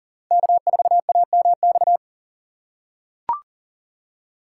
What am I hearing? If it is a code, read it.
Morse code, heard as a keyed tone, about 40 wpm, 700 Hz. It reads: K4AMX
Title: USA Call Signs - 5 Characters - 4x Character Spacing - 40wpm